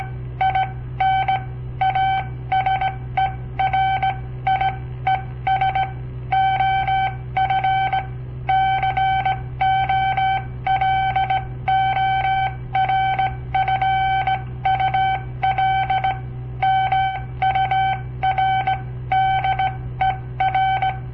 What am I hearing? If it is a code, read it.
Morse code, heard as a keyed tone, about 18 wpm, spaced as for 15 wpm, 770 Hz. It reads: IN A SERIES OF COLORFUL MURDER
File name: papertapes / local / Code Practice-01